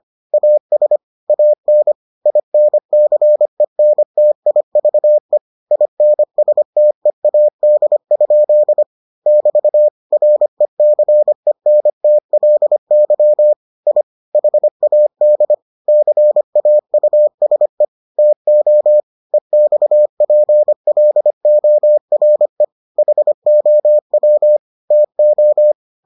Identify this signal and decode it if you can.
Morse code, heard as a keyed tone, about 25 wpm, 600 Hz. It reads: AS AN INCENTIVE INSTEAD? <BT> RECENTLY I HAD CAUSE TO EXPLORE HOW TO